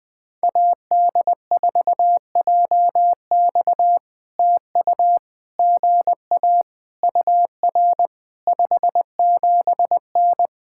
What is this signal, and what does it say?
Morse code, heard as a keyed tone, about 20 wpm, 700 Hz. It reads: AD4JX TU GA UR 57N
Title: POTA QSO Practice - 20wpm